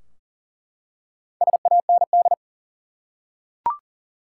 Morse code, heard as a keyed tone, about 40 wpm, 700 Hz. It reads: SAND
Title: Four Letter Words - 40wpm